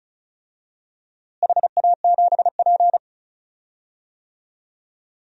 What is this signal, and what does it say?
Morse code, heard as a keyed tone, about 34 wpm, 700 Hz. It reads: HA7P
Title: DXCC-100 Call Signs - 4 Characters - Repeated 3x - 35wpm